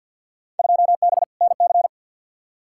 Morse code, heard as a keyed tone, about 52 wpm, 700 Hz. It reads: 2B NC
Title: Field Day Exchanges Encore - 50wpm